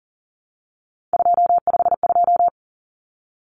Morse code, heard as a keyed tone, about 40 wpm, 700 Hz. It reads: 252
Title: Numbers - 3-Digit 40wpm